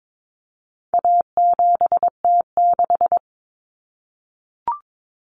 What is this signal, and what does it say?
Morse code, heard as a keyed tone, about 22 wpm, 700 Hz. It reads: A7T6
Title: Common Cut Numbers - 4-Digit 22wpm